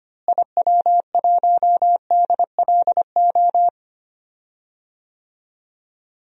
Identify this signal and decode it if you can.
Morse code, heard as a keyed tone, about 25 wpm, 700 Hz. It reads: IW1DLO
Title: DXCC-100 Call Signs - 6 Characters - Repeated 2x - 25wpm